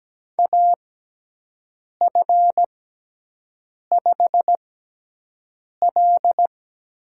Morse code, heard as a keyed tone, about 17 wpm, spaced as for 6 wpm, 700 Hz. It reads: AF5L
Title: USA Call Signs - 4 Characters - 6x Character Spacing - 17wpm